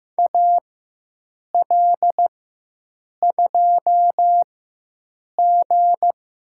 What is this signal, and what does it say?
Morse code, heard as a keyed tone, about 15 wpm, spaced as for 7 wpm, 700 Hz. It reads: AL2G
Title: USA Call Signs - 4 Characters - 4x Character Spacing - 15wpm